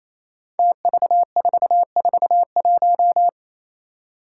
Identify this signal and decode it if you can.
Morse code, heard as a keyed tone, about 28 wpm, 700 Hz. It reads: TV441
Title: Five Challenging Letter-Number Combinations 28wpm